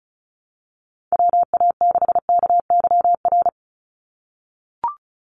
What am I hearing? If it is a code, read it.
Morse code, heard as a keyed tone, about 34 wpm, 700 Hz. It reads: WA6KYR
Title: NA SOTA Chaser Callsigns 35wpm